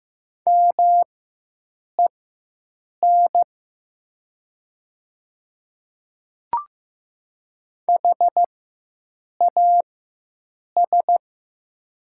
Morse code, heard as a keyed tone, about 15 wpm, spaced as for 7 wpm, 690 Hz. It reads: MEN HAS
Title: 4x Character Spacing - Top 500 Words - 15wpm